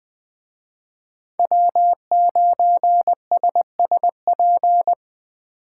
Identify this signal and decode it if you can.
Morse code, heard as a keyed tone, about 19 wpm, 700 Hz. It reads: W9SSP
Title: Speed Racing - USA 1x3 Call Signs 17wpm